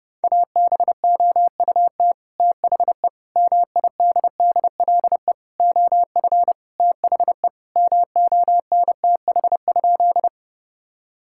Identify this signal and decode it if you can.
Morse code, heard as a keyed tone, about 30 wpm, 700 Hz. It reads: ABOUT THE MIDDLE OF THE MONTH?
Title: Sentences from Top 1000 Words - Review Part 6 - 30wpm